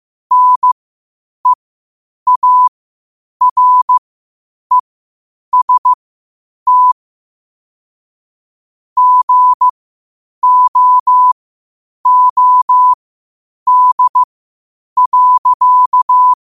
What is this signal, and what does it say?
Morse code, heard as a keyed tone, about 15 wpm, spaced as for 8 wpm, 1000 Hz. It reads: NEAREST GOOD.
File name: qotdfiles / 8wpm